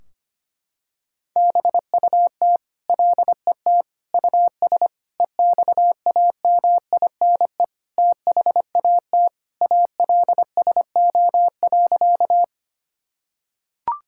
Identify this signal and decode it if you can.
Morse code, heard as a keyed tone, about 25 wpm, 700 Hz. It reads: BUT LET US EXAMINE THAT ALSO.
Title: Sentences from Top 1600 Words 25wpm